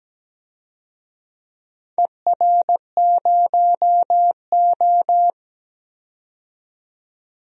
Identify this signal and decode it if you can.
Morse code, heard as a keyed tone, about 17 wpm, 700 Hz. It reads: ER0O
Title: DXCC-100 Call Signs - 4 Characters - 17wpm